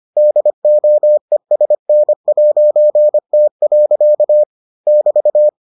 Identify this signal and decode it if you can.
Morse code, heard as a keyed tone, about 25 wpm, 600 Hz. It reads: DOESN'T. <BT>